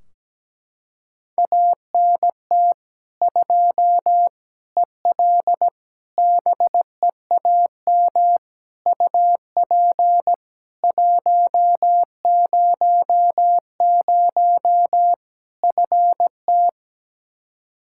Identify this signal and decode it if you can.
Morse code, heard as a keyed tone, about 17 wpm, 700 Hz. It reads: ANT 2 EL BEAM UP 100 FT
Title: Practice QSO - Exchange 2 - Info - 17wpm